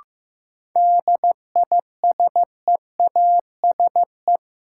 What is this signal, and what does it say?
Morse code, heard as a keyed tone, about 15 wpm, 700 Hz. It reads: DISEASE
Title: Sentences from Top 1000 Words - Review Part 6 - New Word Drill - 15wpm